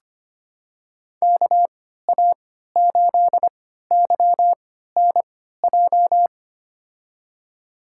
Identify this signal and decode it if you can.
Morse code, heard as a keyed tone, about 25 wpm, spaced as for 14 wpm, 700 Hz. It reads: KA8YNJ
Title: USA Call Signs - 3x Character Spacing - 25wpm